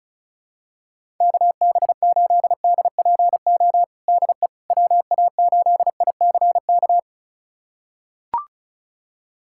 Morse code, heard as a keyed tone, about 36 wpm, 700 Hz. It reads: KB8DPO DE WA8ICK
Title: USA Callsign DE Callsign - 6 Characters - 35wpm